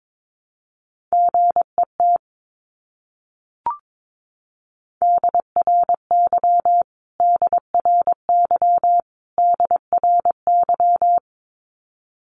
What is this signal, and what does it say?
Morse code, heard as a keyed tone, about 22 wpm, 700 Hz. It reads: GET DRY DRY DRY
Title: Top 500 Words Repeated 22wpm